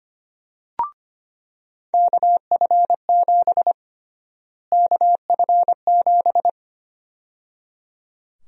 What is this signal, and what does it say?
Morse code, heard as a keyed tone, about 25 wpm, 700 Hz. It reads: KF7 KF7
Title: USA Call Sign Prefixes - Repeated 2x - 25wpm